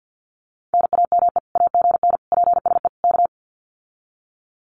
Morse code, heard as a keyed tone, about 50 wpm, 700 Hz. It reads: NAME AGN PSE K
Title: Common QSO Phrases Encore 50wpm